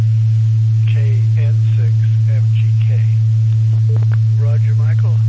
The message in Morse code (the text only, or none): none